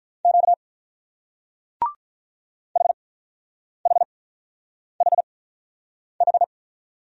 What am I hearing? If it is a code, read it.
Morse code, heard as a keyed tone, about 43 wpm, spaced as for 9 wpm, 690 Hz. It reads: Q HHHH
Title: Speed-Racing Single Letters 35wpm